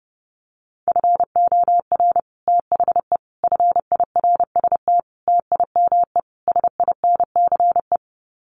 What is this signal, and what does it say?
Morse code, heard as a keyed tone, about 30 wpm, 700 Hz. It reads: FOR THE FIRST TIME SINCE
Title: Sets of 5 Words 30wpm